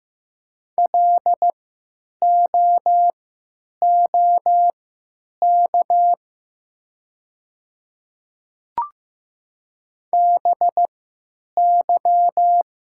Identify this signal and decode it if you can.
Morse code, heard as a keyed tone, about 15 wpm, spaced as for 9 wpm, 700 Hz. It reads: LOOK BY
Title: 3x Character Spacing - Top 300 Words - 15wpm